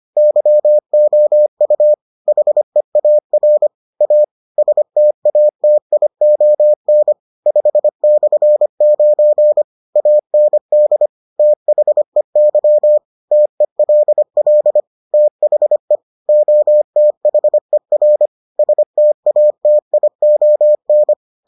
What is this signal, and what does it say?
Morse code, heard as a keyed tone, about 25 wpm, 600 Hz. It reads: YOU HEAR A STATION 5/9 AND THEY TELL THE OTHER STATION